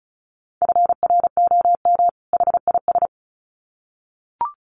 Morse code, heard as a keyed tone, about 34 wpm, 700 Hz. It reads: FROM HIS
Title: Sets of 2 Words 35wpm